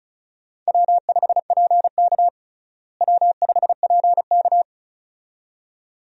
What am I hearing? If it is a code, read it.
Morse code, heard as a keyed tone, about 36 wpm, 700 Hz. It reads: W5PK W5PK
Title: USA Call Signs - 4 Characters - Repeated 2x - 35wpm